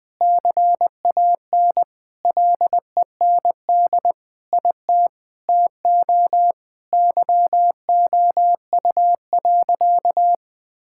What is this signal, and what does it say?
Morse code, heard as a keyed tone, about 21 wpm, 700 Hz. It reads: CAN LEND IT TO YOU.
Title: Sentences with 5-Word Phrases - 20wpm